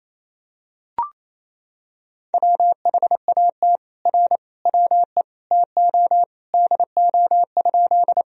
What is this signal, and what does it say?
Morse code, heard as a keyed tone, about 27 wpm, 700 Hz. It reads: WHAT R WE TO DO?
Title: Sentences from Top 100 Words and Abbreviations 28wpm